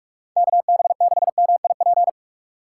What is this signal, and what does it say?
Morse code, heard as a keyed tone, about 46 wpm, 700 Hz. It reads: KB6MIP
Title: USA Call Signs - 6 Characters - 2x Character Spacing - 45wpm